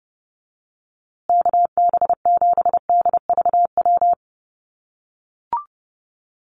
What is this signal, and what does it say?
Morse code, heard as a keyed tone, about 30 wpm, 700 Hz. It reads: KB7DVW
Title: USA Call Signs Repeated 3x 30wpm